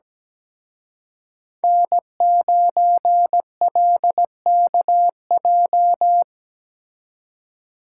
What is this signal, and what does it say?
Morse code, heard as a keyed tone, about 20 wpm, 700 Hz. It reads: N9LKJ
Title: Speed Racing - USA 1x3 Call Signs 17wpm